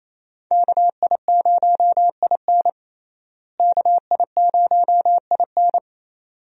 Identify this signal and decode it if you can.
Morse code, heard as a keyed tone, about 28 wpm, 700 Hz. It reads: KI0IN KI0IN
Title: USA Call Signs - 5 Characters - Repeated 2x - 28wpm